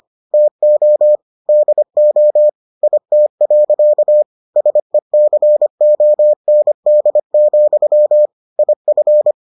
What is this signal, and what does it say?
Morse code, heard as a keyed tone, about 25 wpm, 600 Hz. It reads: TO DO IT. SECOND, IF